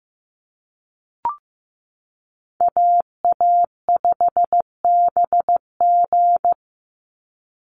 Morse code, heard as a keyed tone, about 15 wpm, 700 Hz. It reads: AA5BG
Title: Call Signs Repeated 3x - 15wpm